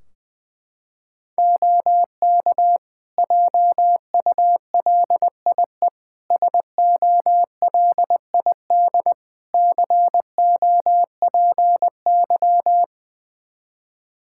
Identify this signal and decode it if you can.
Morse code, heard as a keyed tone, about 21 wpm, 700 Hz. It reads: OK JULIE SOLID COPY
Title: Practice QSO - Exchange 3 - More Info - 20wpm